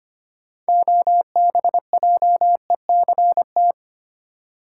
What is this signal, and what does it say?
Morse code, heard as a keyed tone, about 25 wpm, 700 Hz. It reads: OBJECT
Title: Sentences from Top 500 Words - New Word Drill - 25wpm